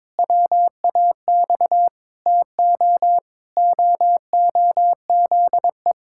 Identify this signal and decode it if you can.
Morse code, heard as a keyed tone, about 22 wpm, 700 Hz. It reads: WAX TO OOZE
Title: Pangram Challenge v2 - 22wpm